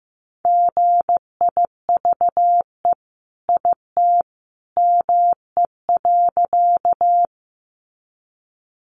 Morse code, heard as a keyed tone, about 15 wpm, 700 Hz. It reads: GIVE IT ME.
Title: Sentences from Top 200 Words 15wpm